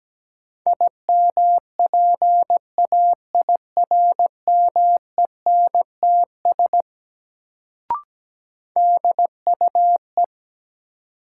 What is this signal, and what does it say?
Morse code, heard as a keyed tone, about 17 wpm, 700 Hz. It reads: IMPAIRMENTS DUE
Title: Sets of 5 Words Encore - New Word Drill - 17wpm